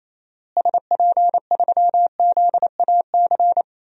Morse code, heard as a keyed tone, about 28 wpm, 700 Hz. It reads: SP3ZAC
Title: DXCC-100 Call Signs - 6 Characters - Repeated 3x - 28wpm